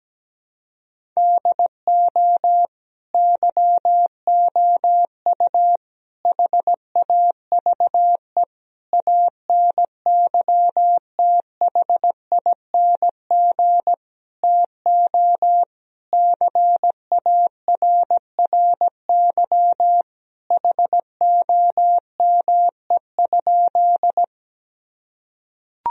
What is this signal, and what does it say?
Morse code, heard as a keyed tone, about 17 wpm, 700 Hz. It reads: DO YOU HAVE ANYTHING TO CARRY HOME?
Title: Sentences from Top 1000 Words - Review Part 1 - 17wpm